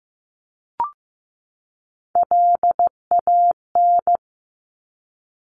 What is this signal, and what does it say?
Morse code, heard as a keyed tone, about 15 wpm, 700 Hz. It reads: LAN
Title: Top Three Letter Combinations in Words 15wpm